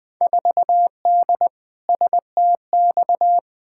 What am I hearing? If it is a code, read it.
Morse code, heard as a keyed tone, about 20 wpm, 700 Hz. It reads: 4D STX
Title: Field Day Exchanges Encore - 20wpm